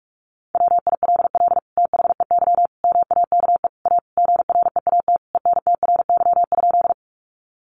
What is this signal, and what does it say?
Morse code, heard as a keyed tone, about 44 wpm, 700 Hz. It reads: WILL THEY MAKE A GREAT ENTRY?
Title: Sentences from Top 1500 Words 45wpm